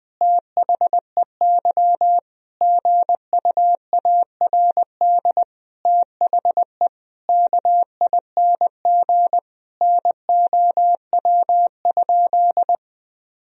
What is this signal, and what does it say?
Morse code, heard as a keyed tone, about 20 wpm, 700 Hz. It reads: THEY GUARD THE KING NOW?
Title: Sentences from Top 1000 Words - Review Part 4 - 20wpm